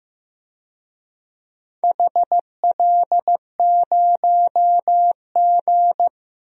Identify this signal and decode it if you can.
Morse code, heard as a keyed tone, about 16 wpm, 700 Hz. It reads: HL0G
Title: DXCC-100 Call Signs - 4 Characters - Repeated 3x - 15wpm